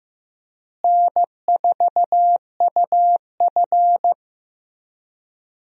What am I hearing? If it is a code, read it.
Morse code, heard as a keyed tone, about 15 wpm, 700 Hz. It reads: N4UF
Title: USA Call Signs - 4 Characters - Repeated 2x - 15wpm